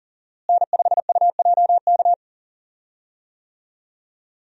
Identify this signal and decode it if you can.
Morse code, heard as a keyed tone, about 40 wpm, 700 Hz. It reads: N5UJK
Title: USA Call Signs - 5 Characters - Repeated 3x - 40wpm